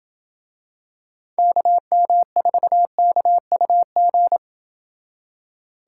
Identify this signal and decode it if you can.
Morse code, heard as a keyed tone, about 28 wpm, 700 Hz. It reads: KM4KUG